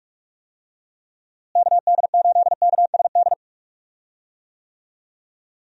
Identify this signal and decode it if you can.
Morse code, heard as a keyed tone, about 46 wpm, 700 Hz. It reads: KD8KSD
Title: USA Call Signs - 6 Characters - Repeated 2x - 45wpm